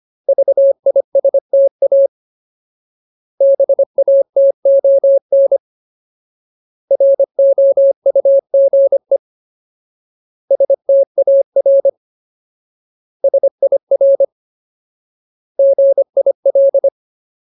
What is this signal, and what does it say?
Morse code, heard as a keyed tone, about 25 wpm, 550 Hz. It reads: VISTA BATON ROUGE STAR SIR GIL
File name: Ultimate Word List / 25WPM Ultimate